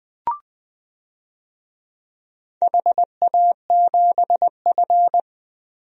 Morse code, heard as a keyed tone, about 21 wpm, 700 Hz. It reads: HA7F
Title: DXCC-100 Call Signs - 4 Characters - 20wpm